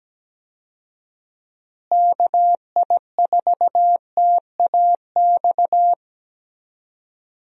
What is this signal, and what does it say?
Morse code, heard as a keyed tone, about 17 wpm, 700 Hz. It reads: KI4TAX
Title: USA Call Signs - 6 Characters - Repeated 3x - 17wpm